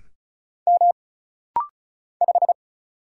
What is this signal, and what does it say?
Morse code, heard as a keyed tone, about 36 wpm, 700 Hz. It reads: M 5
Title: Single Letter-Number - Rapid-Fire 35wpm